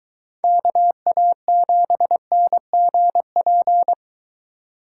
Morse code, heard as a keyed tone, about 23 wpm, 700 Hz. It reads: KA7NGP